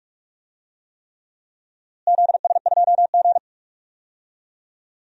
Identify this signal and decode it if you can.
Morse code, heard as a keyed tone, about 44 wpm, 700 Hz. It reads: ZS2G
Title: DXCC-100 Call Signs - 4 Characters - Repeated 3x - 45wpm